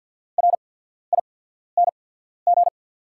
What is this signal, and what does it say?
Morse code, heard as a keyed tone, about 48 wpm, spaced as for 14 wpm, 690 Hz. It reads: RING
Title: 8x Character Spacing - Top 500 Words - 50wpm